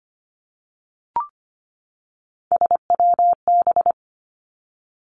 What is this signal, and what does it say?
Morse code, heard as a keyed tone, about 25 wpm, 700 Hz. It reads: SWB
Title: Three Challenging Letter-Number Combinations 25wpm